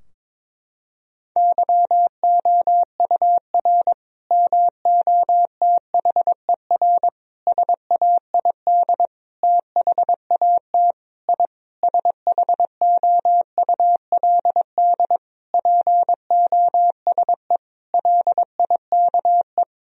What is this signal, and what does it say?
Morse code, heard as a keyed tone, about 22 wpm, 700 Hz. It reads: YOUR MOTHER SAID THAT I SHOULD POSE LIKE
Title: Sentences from Top 1000 Words - Review Part 2 - 22wpm